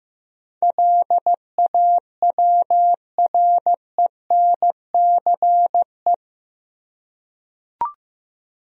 Morse code, heard as a keyed tone, about 15 wpm, 700 Hz. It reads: LAWRENCE
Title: USA Common Names - 15wpm